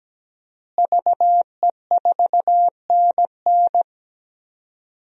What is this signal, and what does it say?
Morse code, heard as a keyed tone, about 17 wpm, 700 Hz. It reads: VE4NN
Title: DXCC-100 Call Signs - 5 Characters - Repeated 2x - 17wpm